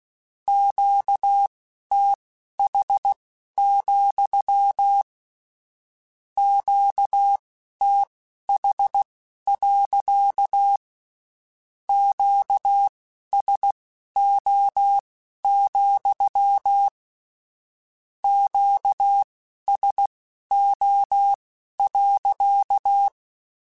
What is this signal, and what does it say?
Morse code, heard as a keyed tone, about 16 wpm, spaced as for 12 wpm, 800 Hz. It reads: QTH, QTH. QSO, QSO.
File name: Q Calls / Q Calls-mp3s / Q Calls sppc @15